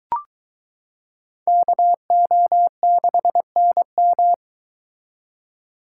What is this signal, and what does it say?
Morse code, heard as a keyed tone, about 23 wpm, 700 Hz. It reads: KO6NM